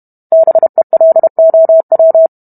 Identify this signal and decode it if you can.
Morse code, heard as a keyed tone, about 32 wpm, 650 Hz. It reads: BELOW